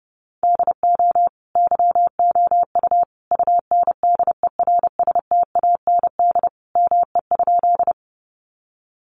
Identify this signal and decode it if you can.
Morse code, heard as a keyed tone, about 30 wpm, 700 Hz. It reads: DO YOU UNDERSTAND ME?
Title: Sentences from Top 600 Words 30wpm